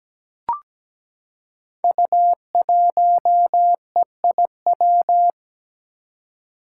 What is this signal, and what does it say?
Morse code, heard as a keyed tone, about 17 wpm, 700 Hz. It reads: U1EIW
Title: Five Challenging Letter-Number Combinations 17wpm